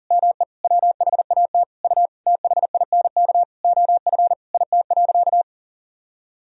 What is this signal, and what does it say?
Morse code, heard as a keyed tone, about 40 wpm, 700 Hz. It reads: ME WHAT U THINK OF IT.
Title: Sentences from Top 200 Words and Abbreviations 40wpm